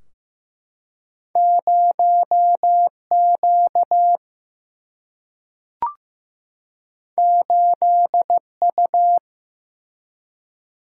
Morse code, heard as a keyed tone, about 15 wpm, 700 Hz. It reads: 0Q 8U
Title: Single-Letter Single-Number 15wpm